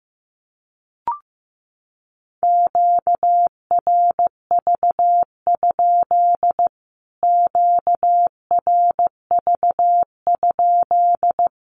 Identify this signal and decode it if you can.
Morse code, heard as a keyed tone, about 15 wpm, 700 Hz. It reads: QRV? QRV?
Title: Q-Codes Repeated 3x 15wpm